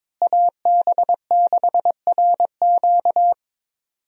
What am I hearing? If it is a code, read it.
Morse code, heard as a keyed tone, about 22 wpm, 700 Hz. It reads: AB6RQ
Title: USA Call Signs - 5 Characters - Repeated 3x - 22wpm